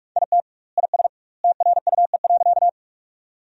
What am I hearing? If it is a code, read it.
Morse code, heard as a keyed tone, about 41 wpm, 700 Hz. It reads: IT IS TRUE.
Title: 2x Word-Spacing - Sentences from Top 400 Words - 45wpm